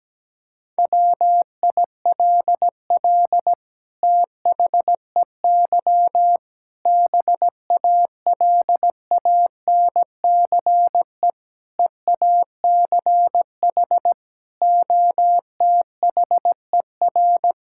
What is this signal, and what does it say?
Morse code, heard as a keyed tone, about 17 wpm, 700 Hz. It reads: WILL THEY BALANCE EACH OTHER
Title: Sentences from Top 1000 Words - Review Part 6 - 17wpm